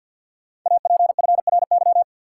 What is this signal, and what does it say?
Morse code, heard as a keyed tone, about 48 wpm, 700 Hz. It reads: APFRY
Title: Code Groups - 5 Letters - 50wpm